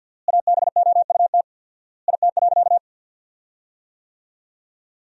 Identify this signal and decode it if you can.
Morse code, heard as a keyed tone, about 52 wpm, 700 Hz. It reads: ABOUT IT.
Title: 4x Word-Spacing - Sentences from Top 500 Words - 50wpm